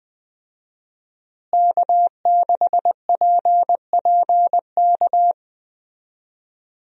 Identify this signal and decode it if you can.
Morse code, heard as a keyed tone, about 21 wpm, 700 Hz. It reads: K6PPK
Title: USA Call Signs - 5 Characters - Repeated 3x - 20wpm